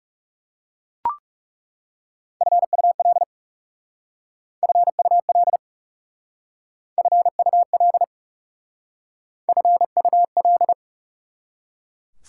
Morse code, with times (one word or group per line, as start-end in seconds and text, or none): 2.4-3.2 FUL
4.6-5.6 FUL
7.0-8.0 FUL
9.5-10.7 FUL